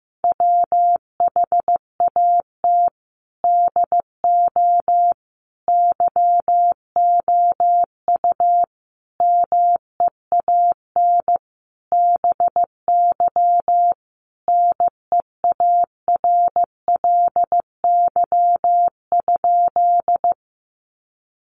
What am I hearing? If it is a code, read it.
Morse code, heard as a keyed tone, about 15 wpm, 700 Hz. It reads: WHAT DO YOU MEAN BY NEARLY?
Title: Sentences from Top 1300 Words 15wpm